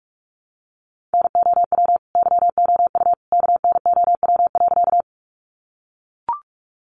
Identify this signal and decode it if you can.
Morse code, heard as a keyed tone, about 44 wpm, 700 Hz. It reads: NOW YOU KNOW.
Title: Sentences from Top 100 Words 45wpm